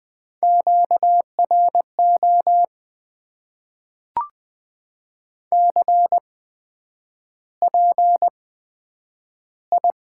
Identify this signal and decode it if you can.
Morse code, heard as a keyed tone, about 20 wpm, spaced as for 9 wpm, 700 Hz. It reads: QRO CPI
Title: Top 90 QSO Elements - 8x Character Spacing - 20wpm